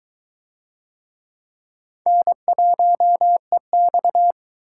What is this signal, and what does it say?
Morse code, heard as a keyed tone, about 21 wpm, 700 Hz. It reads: N1EX